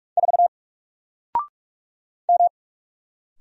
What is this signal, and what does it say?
Morse code, heard as a keyed tone, about 44 wpm, 700 Hz. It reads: 4 M
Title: Single Character (Letter:Number:Punctuation:Prosign) 45wpm